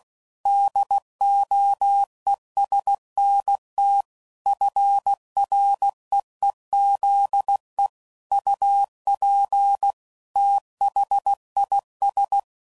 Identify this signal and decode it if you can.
Morse code, heard as a keyed tone, about 16 wpm, 800 Hz. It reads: DOESNT FREEZE UP THIS
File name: CW Code Groups @15WPM / QSOs / WD0FBT